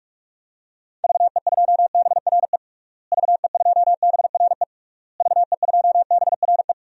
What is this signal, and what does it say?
Morse code, heard as a keyed tone, about 46 wpm, 700 Hz. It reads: VE2BRE VE2BRE VE2BRE
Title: DXCC-100 Call Signs - 6 Characters - Repeated 3x - 45wpm